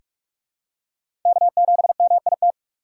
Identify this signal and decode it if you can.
Morse code, heard as a keyed tone, about 47 wpm, 690 Hz. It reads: K7MIT
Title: USA Call Signs - 5 Characters - 2x Character Spacing - 45wpm